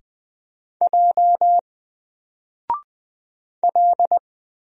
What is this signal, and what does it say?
Morse code, heard as a keyed tone, about 20 wpm, 700 Hz. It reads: J L
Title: Single Character (Letter:Number:Punctuation:Prosign) 20wpm